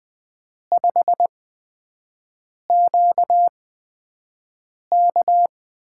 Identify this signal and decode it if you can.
Morse code, heard as a keyed tone, about 20 wpm, spaced as for 5 wpm, 700 Hz. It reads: 5QK
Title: USA Call Signs - 8x Character Spacing - 20wpm